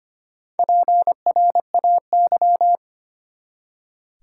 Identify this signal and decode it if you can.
Morse code, heard as a keyed tone, about 25 wpm, 700 Hz. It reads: PRAY
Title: Four Letter Words - 25wpm